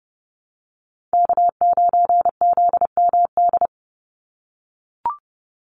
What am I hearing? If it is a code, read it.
Morse code, heard as a keyed tone, about 30 wpm, 700 Hz. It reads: K9ZMD
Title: NA SOTA Chaser Callsigns 30wpm